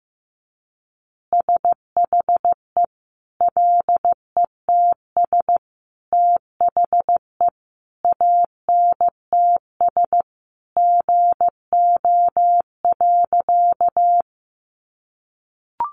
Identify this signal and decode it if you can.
Morse code, heard as a keyed tone, about 15 wpm, 700 Hz. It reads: SHE LETS THE ANTS GO.
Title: Easy Phrases 15wpm